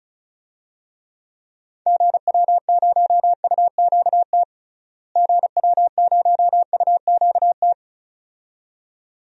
Morse code, heard as a keyed tone, about 34 wpm, 700 Hz. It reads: GW0UQT GW0UQT
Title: DXCC-100 Call Signs - 6 Characters - Repeated 2x - 35wpm